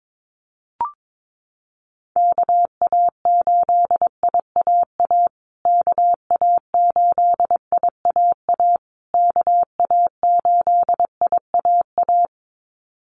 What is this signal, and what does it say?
Morse code, heard as a keyed tone, about 22 wpm, 700 Hz. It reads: KA8IAA KA8IAA KA8IAA
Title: USA Call Signs Repeated 3x 22wpm